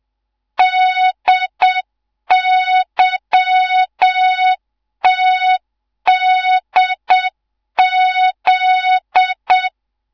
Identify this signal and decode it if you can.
Morse code, heard as a keyed tone, about 7 wpm, 3700 Hz. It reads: DYTDZ